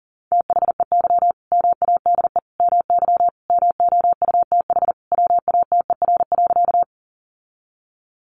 Morse code, heard as a keyed tone, about 40 wpm, 700 Hz. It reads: THEY MADE MY MOUTH WATER.
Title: Sentences from Top 1500 Words 40wpm